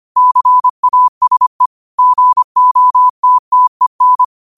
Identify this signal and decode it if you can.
Morse code, heard as a keyed tone, about 25 wpm, 1000 Hz. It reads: CASE GOTTEN
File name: qotdfiles / 25wpm